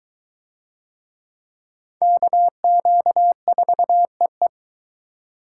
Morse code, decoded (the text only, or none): KQ4EE